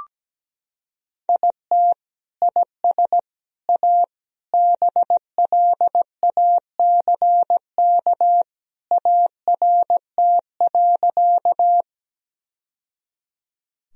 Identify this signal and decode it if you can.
Morse code, heard as a keyed tone, about 17 wpm, 700 Hz. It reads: IT IS A BLACK ART.
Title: Sentences from Top 1000 Words - Review Part 3 - 17wpm